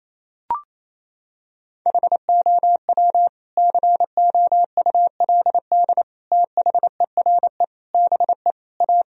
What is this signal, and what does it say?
Morse code, heard as a keyed tone, about 28 wpm, 700 Hz. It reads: HOW COULD THERE BE A
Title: Sentences from Top 1000 Words - Review Part 3 - 28wpm